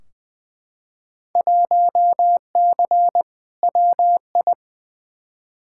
Morse code, heard as a keyed tone, about 20 wpm, 700 Hz. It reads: 1C WI
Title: Field Day Exchanges - 20wpm